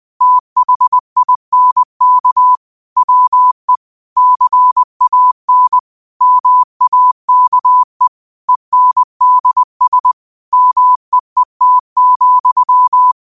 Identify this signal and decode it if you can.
Morse code, heard as a keyed tone, about 20 wpm, 1000 Hz. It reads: THINK WE CAN MAKE ENDS MEET,
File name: qotdfiles / 20wpm